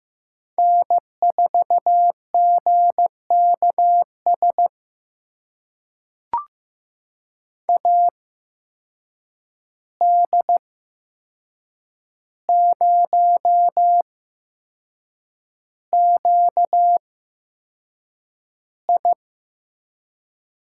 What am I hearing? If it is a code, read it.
Morse code, heard as a keyed tone, about 15 wpm, spaced as for 6 wpm, 700 Hz. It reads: N4GKS AD0QI
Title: USA Call Signs - 5 Characters - 8x Character Spacing - 15wpm